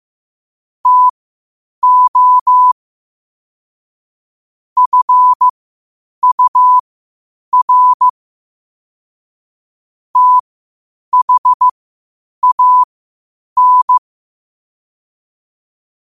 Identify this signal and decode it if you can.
Morse code, heard as a keyed tone, about 15 wpm, spaced as for 8 wpm, 1000 Hz. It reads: TO FUR THAN
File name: qotdfiles / 8wpm